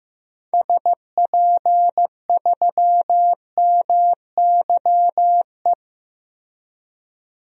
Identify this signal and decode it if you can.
Morse code, heard as a keyed tone, about 16 wpm, 700 Hz. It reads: SP3MYE
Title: DXCC-100 Call Signs - 6 Characters - Repeated 3x - 15wpm